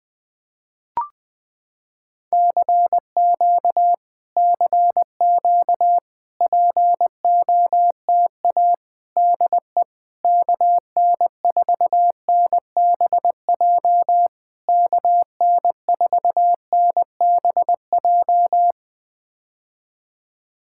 Morse code, heard as a keyed tone, about 21 wpm, 700 Hz. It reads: CQ CQ POTA DE KN4NBJ KN4NBJ
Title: POTA QSO Practice - 20wpm